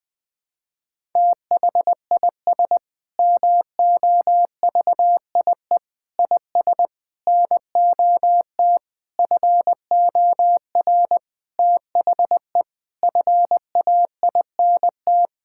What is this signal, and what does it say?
Morse code, heard as a keyed tone, about 20 wpm, 700 Hz. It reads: THIS MOVIE IS NOT FOR THE FAINT
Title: Sentences with 7-Word Phrases - Spoken Beginning - 20wpm